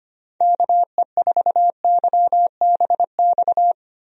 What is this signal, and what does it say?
Morse code, heard as a keyed tone, about 25 wpm, 700 Hz. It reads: KE4YBX
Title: USA Call Signs - 6 Characters - 25wpm